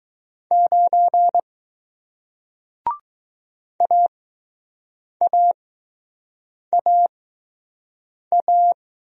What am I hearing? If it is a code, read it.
Morse code, heard as a keyed tone, about 20 wpm, spaced as for 6 wpm, 700 Hz. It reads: 9 AAAA